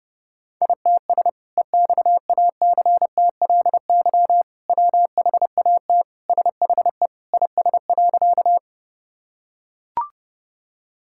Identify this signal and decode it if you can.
Morse code, heard as a keyed tone, about 30 wpm, 700 Hz. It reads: ITS EXACTLY WHAT SHE IS.
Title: Sentences from Top 1000 Words - Review Part 6 - 30wpm